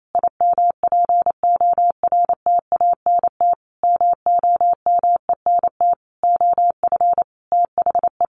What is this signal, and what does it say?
Morse code, heard as a keyed tone, about 28 wpm, 700 Hz. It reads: IMPORTANT MOMENT OF THE
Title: Sentences from Top 1500 Words 28wpm